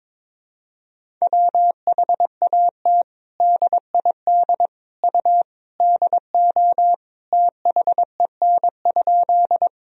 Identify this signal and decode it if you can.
Morse code, heard as a keyed tone, about 22 wpm, 700 Hz. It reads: WHAT DID U DO THEN?
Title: Sentences from Top 100 Words and Abbreviations 22wpm